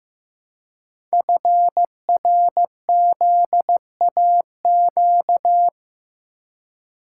Morse code, heard as a keyed tone, about 15 wpm, 700 Hz. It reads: FRZAQ
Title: Code Groups - 5 Letters - 15wpm